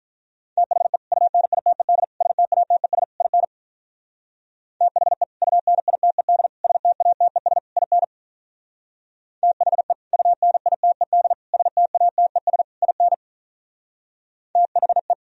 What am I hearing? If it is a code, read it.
Morse code, heard as a keyed tone, about 46 wpm, 700 Hz. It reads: THE UNITED STATES IN THE UNITED STATES IN THE UNITED STATES IN THE